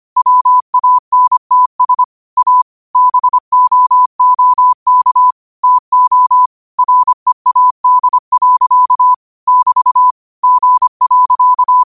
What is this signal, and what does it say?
Morse code, heard as a keyed tone, about 25 wpm, 1000 Hz. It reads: WANTS A BOOK TO READ. <BT> G.
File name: qotdfiles / 25wpm